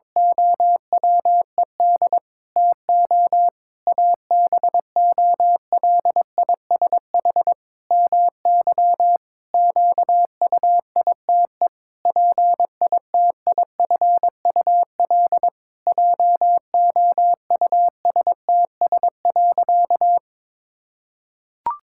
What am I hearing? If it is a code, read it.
Morse code, heard as a keyed tone, about 22 wpm, 700 Hz. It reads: OWED TO ABOLISH MY QUITE PITIFUL JOUSTS.
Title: Pangram Challenge v2 - 22wpm